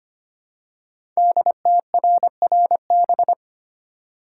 Morse code, heard as a keyed tone, about 25 wpm, 700 Hz. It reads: DTRRB
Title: Code Groups - 5 Letters - 25wpm